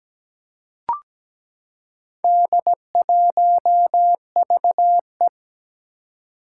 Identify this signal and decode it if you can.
Morse code, heard as a keyed tone, about 17 wpm, 700 Hz. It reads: D1VE
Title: Four Challenging Letter-Number Combinations 17wpm